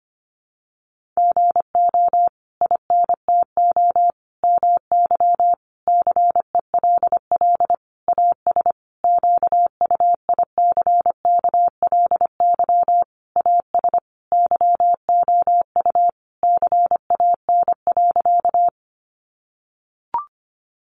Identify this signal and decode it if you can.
Morse code, heard as a keyed tone, about 25 wpm, 700 Hz. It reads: GO INTO MY CELL AS QUICKLY AS YOU CAN.
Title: Sentences from Top 1100 Words 25wpm